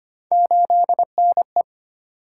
Morse code, heard as a keyed tone, about 25 wpm, 700 Hz. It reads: 8NE
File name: Three Letter-Number Combinations - 25wpm